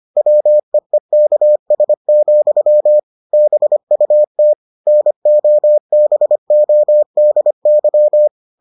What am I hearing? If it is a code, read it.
Morse code, heard as a keyed tone, about 25 wpm, 600 Hz. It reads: WEEKS, BUT NOBODY